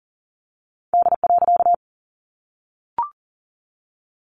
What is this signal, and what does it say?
Morse code, heard as a keyed tone, about 40 wpm, 700 Hz. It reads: D.
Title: A Sound Of Thunder 40wpm